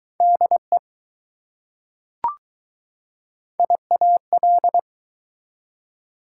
Morse code, heard as a keyed tone, about 20 wpm, 700 Hz. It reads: DE IAL